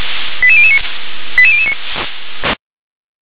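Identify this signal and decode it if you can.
Five-tone selective call (ZVEI), heard as quick runs of five tones, tones of about 70 ms. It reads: 88009 88009
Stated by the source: Five-tone ZVEI selective call received on an SDR.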